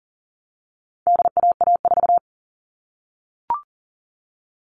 Morse code, heard as a keyed tone, about 40 wpm, 700 Hz. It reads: DAA4
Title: All Cut Numbers - 4-Digit 40wpm